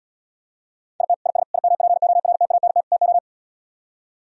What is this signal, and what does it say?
Morse code, heard as a keyed tone, about 77 wpm, 700 Hz. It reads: IT IS IMPORTANT TO